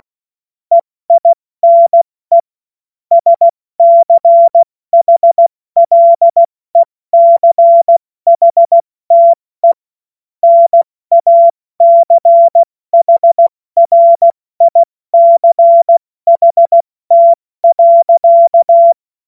Morse code, heard as a keyed tone, about 16 wpm, 680 Hz. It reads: EINE SCHLECHTE NACHRICHT.